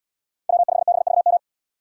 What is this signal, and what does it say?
Morse code, heard as a keyed tone, about 77 wpm, 700 Hz. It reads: W5BVM